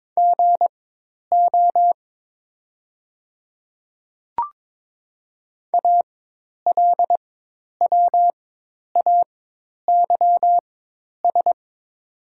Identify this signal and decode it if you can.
Morse code, heard as a keyed tone, about 22 wpm, spaced as for 10 wpm, 700 Hz. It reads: GO ALWAYS
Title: 4x Character Spacing - Top 300 Words - 22wpm